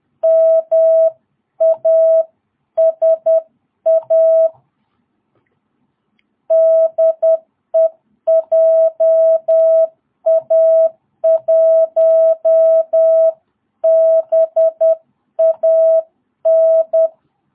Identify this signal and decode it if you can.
Morse code, heard as a keyed tone, about 10 wpm, 660 Hz. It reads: MASA DEJA1BAN
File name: jh8cbh / 2022cw